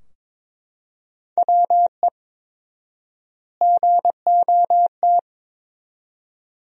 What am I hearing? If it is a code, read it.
Morse code, heard as a keyed tone, about 22 wpm, 700 Hz. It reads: WE GOT
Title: 4x Word-Spacing - Sentences from Top 300 Words - 22wpm